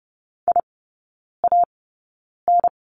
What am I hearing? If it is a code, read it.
Morse code, heard as a keyed tone, about 30 wpm, spaced as for 9 wpm, 700 Hz. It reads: IAN
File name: Lesson 16e - Letter c - Three Characters 30wpm